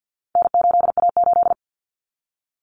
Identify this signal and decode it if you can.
Morse code, heard as a keyed tone, about 50 wpm, 700 Hz. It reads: N8A8
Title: Common Cut Numbers - 4-Digit 50wpm